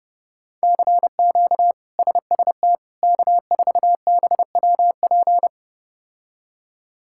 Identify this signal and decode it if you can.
Morse code, heard as a keyed tone, about 30 wpm, 700 Hz. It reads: CQ SST K4BWP
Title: K1USN SST QSO Practice - 30wpm